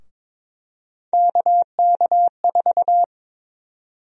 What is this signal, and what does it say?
Morse code, heard as a keyed tone, about 21 wpm, 700 Hz. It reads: KK4
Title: USA Call Sign Prefixes - 22wpm